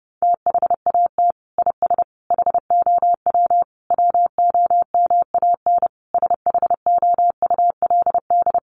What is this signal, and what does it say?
Morse code, heard as a keyed tone, about 30 wpm, 700 Hz. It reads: THAT IS HOW WOMAN SHOULD
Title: Sentences from Top 1100 Words 30wpm